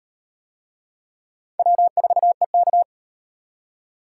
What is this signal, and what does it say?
Morse code, heard as a keyed tone, about 38 wpm, 700 Hz. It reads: W4EK